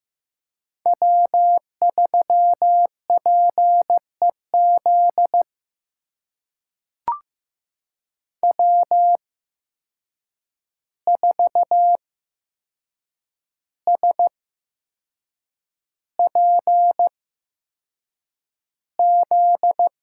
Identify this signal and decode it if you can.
Morse code, heard as a keyed tone, about 15 wpm, spaced as for 6 wpm, 700 Hz. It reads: W3PEZ W4SPZ
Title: USA Call Signs - 5 Characters - 8x Character Spacing - 15wpm